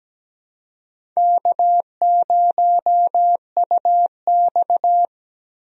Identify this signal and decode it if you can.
Morse code, heard as a keyed tone, about 17 wpm, 700 Hz. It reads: K0UX
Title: USA Call Signs - 4 Characters - 17wpm